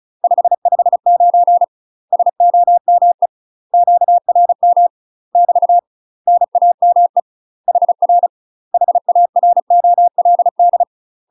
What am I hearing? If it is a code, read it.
Morse code, heard as a keyed tone, about 36 wpm, 700 Hz. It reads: 559 SOME QRM <BT> NAME HR HAROLD
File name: UK MASTER QSOs TRAINING / 35WPM / UK MASTER QSOs 0011